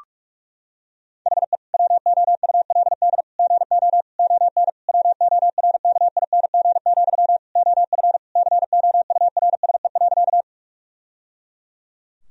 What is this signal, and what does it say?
Morse code, heard as a keyed tone, about 46 wpm, 700 Hz. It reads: HE WOULD GO ON WORKING, OF COURSE.
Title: Sentences from Top 1000 Words - Review Part 4 - 45wpm